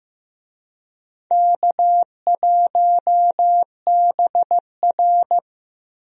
Morse code, heard as a keyed tone, about 15 wpm, 700 Hz. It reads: K1BR
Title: USA Call Signs - 4 Characters - Repeated 3x - 15wpm